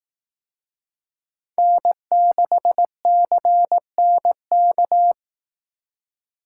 Morse code, heard as a keyed tone, about 18 wpm, 700 Hz. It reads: N6CNK
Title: Speed Racing - USA 1x3 Call Signs 15wpm